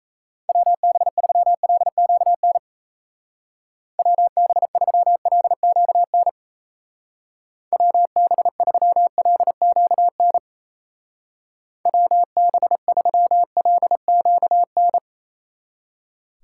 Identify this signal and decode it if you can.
Morse code, heard as a keyed tone, about 36 wpm, 700 Hz. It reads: WB3LQN WB3LQN WB3LQN WB3LQN